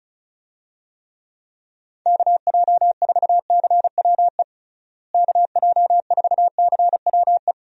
Morse code, heard as a keyed tone, about 36 wpm, 700 Hz. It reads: KJ4CWE KJ4CWE
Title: USA Call Signs - 6 Characters - Repeated 2x - 35wpm